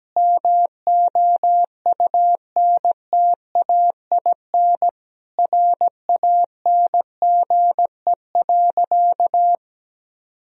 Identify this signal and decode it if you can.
Morse code, heard as a keyed tone, about 17 wpm, 700 Hz. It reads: MOUNTAIN RANGE.
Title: Sentences from Top 1000 Words - Review Part 4 - 17wpm